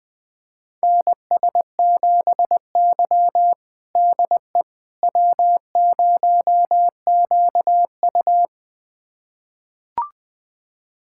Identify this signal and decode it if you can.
Morse code, heard as a keyed tone, about 21 wpm, 700 Hz. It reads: NS7Y DE W0QU
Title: USA Callsign DE Callsign - 4 Characters - 20wpm